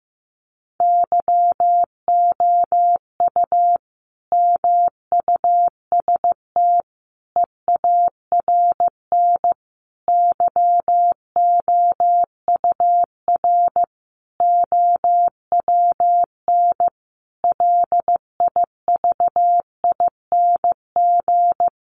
Morse code, heard as a keyed tone, about 15 wpm, 700 Hz. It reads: YOU MUST EARN YOUR OWN LIVING
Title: Sentences from Top 1500 Words 15wpm